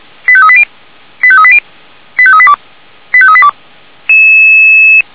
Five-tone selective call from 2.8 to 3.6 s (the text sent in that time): 85382